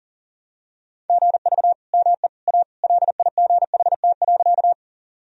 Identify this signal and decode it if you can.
Morse code, heard as a keyed tone, about 40 wpm, 700 Hz. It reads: GV ME A LIGHT.
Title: Sentences from Top 200 Words and Abbreviations 40wpm